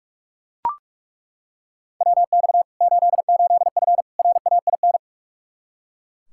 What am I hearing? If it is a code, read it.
Morse code, heard as a keyed tone, about 41 wpm, 700 Hz. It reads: WX 88F RAIN
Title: Practice QSO - Exchange 2 - Info - 45wpm